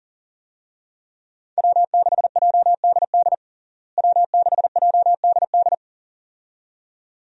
Morse code, heard as a keyed tone, about 40 wpm, 700 Hz. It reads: W6JDD W6JDD
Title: USA Call Signs - 5 Characters - Repeated 2x - 40wpm